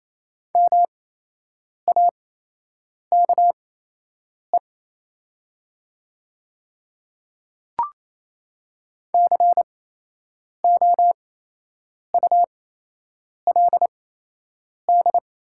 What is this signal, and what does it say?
Morse code, heard as a keyed tone, about 28 wpm, spaced as for 8 wpm, 700 Hz. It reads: MAKE COULD
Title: 8x Character Spacing - Top 500 Words - 28wpm